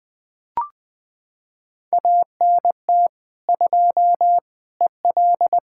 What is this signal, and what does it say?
Morse code, heard as a keyed tone, about 20 wpm, 700 Hz. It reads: ANT 2 EL
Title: Practice QSO - Exchange 2 - Info - 20wpm